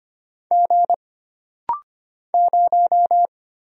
Morse code, heard as a keyed tone, about 25 wpm, 700 Hz. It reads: G 0
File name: Single Character (Letter:Number:Punctuation:Prosign) - Rapid-Fire - 25wpm